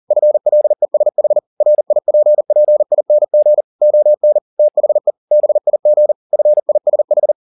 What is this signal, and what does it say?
Morse code, heard as a keyed tone, about 40 wpm, 600 Hz. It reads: FLESH RIPPING ON THE BIG FISH
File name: the old man and the sea 2 40wpm